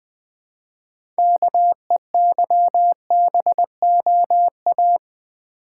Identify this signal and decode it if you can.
Morse code, heard as a keyed tone, about 20 wpm, 700 Hz. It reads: KEYBOA
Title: Build A Word - 2 Letters At A Time - 8-Letter Words - 20wpm